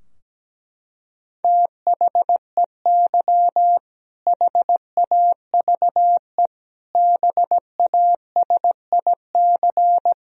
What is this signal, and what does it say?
Morse code, heard as a keyed tone, about 17 wpm, 700 Hz. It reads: THEY HAVE BASIC
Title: Sentences from Top 1000 Words - Review Part 3 - 17wpm